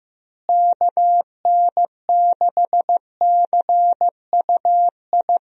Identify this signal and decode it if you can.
Morse code, heard as a keyed tone, about 15 wpm, 700 Hz. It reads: KN6CUI
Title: USA Call Signs - 6 Characters - Repeated 3x - 15wpm